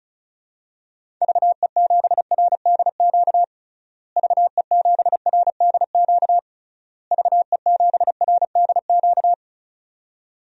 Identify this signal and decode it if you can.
Morse code, heard as a keyed tone, about 34 wpm, 700 Hz. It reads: VE7RDQ VE7RDQ VE7RDQ
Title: DXCC-100 Call Signs - 6 Characters - Repeated 3x - 35wpm